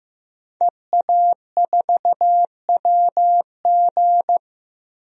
Morse code, heard as a keyed tone, about 15 wpm, 700 Hz. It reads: EA4WG
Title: DXCC-100 Call Signs - 5 Characters - Repeated 3x - 15wpm